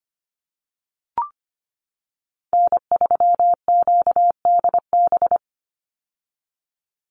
Morse code, heard as a keyed tone, about 25 wpm, 700 Hz. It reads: N3QDB
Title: USA Call Signs 25wpm